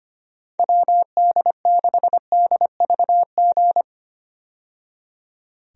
Morse code, heard as a keyed tone, about 25 wpm, 700 Hz. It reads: WD6DVG
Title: USA Call Signs - 6 Characters - 25wpm